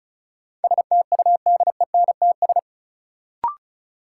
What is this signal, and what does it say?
Morse code, heard as a keyed tone, about 36 wpm, 700 Hz. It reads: STUDENTS
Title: Sets of 5 Words Encore - New Word Drill - 35wpm